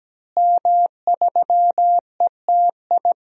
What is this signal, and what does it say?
Morse code, heard as a keyed tone, about 17 wpm, 700 Hz. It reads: M3ETI
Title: DXCC-100 Call Signs - 5 Characters - Repeated 3x - 17wpm